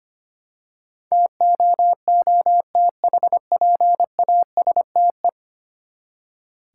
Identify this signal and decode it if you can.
Morse code, heard as a keyed tone, about 25 wpm, 700 Hz. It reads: TOOTHPASTE
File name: practice / 25wpm / Easy Words with Hints - 25wpm